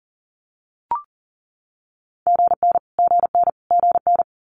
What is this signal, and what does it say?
Morse code, heard as a keyed tone, about 38 wpm, 700 Hz. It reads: GN GN GN
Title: Multi-letter Phonemes Repeated 3x 40wpm